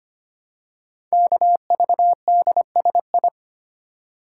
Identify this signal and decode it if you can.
Morse code, heard as a keyed tone, about 25 wpm, 700 Hz. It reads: KVDSI
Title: Code Groups - 5 Letters - 25wpm